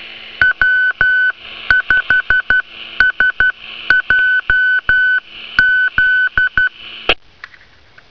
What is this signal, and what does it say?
Morse code, heard as a keyed tone, about 12 wpm, 1500 Hz. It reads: W5SJZ
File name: temp / N5LDI / beacon 16 07 31z